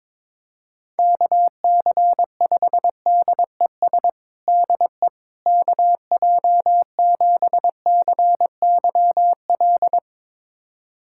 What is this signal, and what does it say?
Morse code, heard as a keyed tone, about 22 wpm, 700 Hz. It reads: KC5DES DE KJ7CYL
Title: USA Callsign DE Callsign - 6 Characters - 22wpm